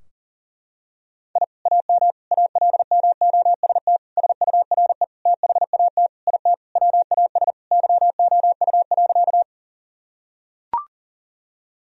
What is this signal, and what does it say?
Morse code, heard as a keyed tone, about 40 wpm, 700 Hz. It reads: I AM ALMOST SURE THAT IT WAS YOU.
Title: Sentences from Top 1000 Words - Review Part 4 - 40wpm